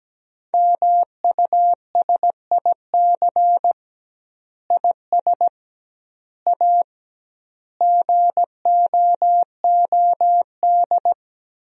Morse code, heard as a keyed tone, about 17 wpm, 700 Hz. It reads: MUSIC IS A GOOD
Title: 2x Word-Spacing - Sentences from Top 300 Words - 17wpm